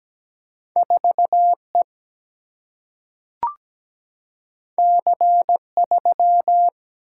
Morse code, heard as a keyed tone, about 17 wpm, 700 Hz. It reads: 4E C3
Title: Single-Letter Single-Number 17wpm